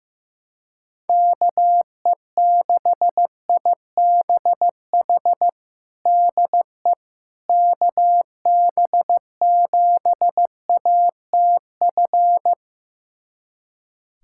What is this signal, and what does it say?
Morse code, heard as a keyed tone, about 15 wpm, 700 Hz. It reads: KE6IBH DE KB7ATF
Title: USA Callsign DE Callsign - 6 Characters - 15wpm